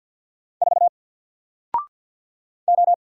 Morse code, heard as a keyed tone, about 50 wpm, 700 Hz. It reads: 4 O
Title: Single Letter-Number Alt 50wpm